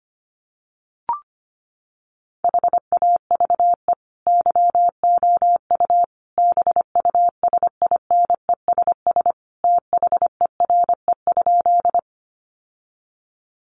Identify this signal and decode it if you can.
Morse code, heard as a keyed tone, about 25 wpm, 700 Hz. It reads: HAVE YOU BUSINESS THERE?
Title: Sentences from Top 600 Words 25wpm